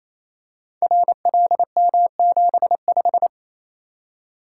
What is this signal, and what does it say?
Morse code, heard as a keyed tone, about 27 wpm, 700 Hz. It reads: RLM75
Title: Code Groups - 5 Letter:Numbers - 28wpm